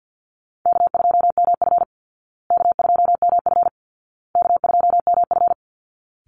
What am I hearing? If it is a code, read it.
Morse code, heard as a keyed tone, about 50 wpm, 700 Hz. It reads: K2MF K2MF K2MF
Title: Easy Call Signs Repeated 3x - 50wpm